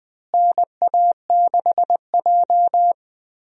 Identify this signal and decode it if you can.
Morse code, heard as a keyed tone, about 20 wpm, 700 Hz. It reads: NA6J
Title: USA Call Signs - 4 Characters - Repeated 3x - 20wpm